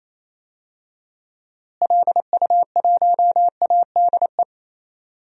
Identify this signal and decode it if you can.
Morse code, heard as a keyed tone, about 28 wpm, 700 Hz. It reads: LU1ADE
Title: DXCC-100 Call Signs - 6 Characters - Repeated 2x - 28wpm